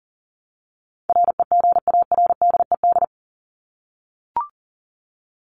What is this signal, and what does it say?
Morse code, heard as a keyed tone, about 40 wpm, 700 Hz. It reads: REGARDED
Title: ED Suffix - Top 100 Words with Base Words 40wpm